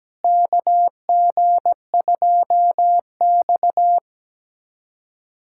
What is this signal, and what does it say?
Morse code, heard as a keyed tone, about 17 wpm, 700 Hz. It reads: KG2X
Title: USA Callsign DE Callsign - 4 Characters - 17wpm